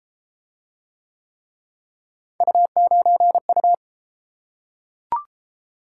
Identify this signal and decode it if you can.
Morse code, heard as a keyed tone, about 32 wpm, 700 Hz. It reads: U9U